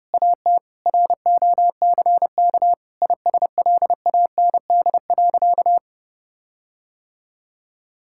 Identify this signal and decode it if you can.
Morse code, heard as a keyed tone, about 30 wpm, 700 Hz. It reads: AT ROCK ISLAND.
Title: Sentences from Top 1600 Words 30wpm